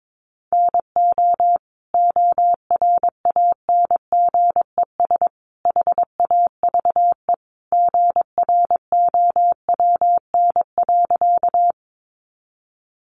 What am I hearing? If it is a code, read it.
Morse code, heard as a keyed tone, about 22 wpm, 700 Hz. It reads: NO ORANGES HAVE GROWN.
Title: Easy Phrases 22wpm